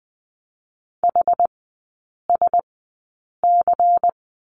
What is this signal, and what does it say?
Morse code, heard as a keyed tone, about 20 wpm, spaced as for 8 wpm, 700 Hz. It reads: HSC
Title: 100 Lesson 17e - Letter u - Three Characters 20wpm